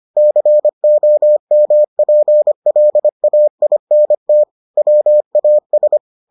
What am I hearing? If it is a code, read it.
Morse code, heard as a keyed tone, about 25 wpm, 600 Hz. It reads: COMPLAINT WAS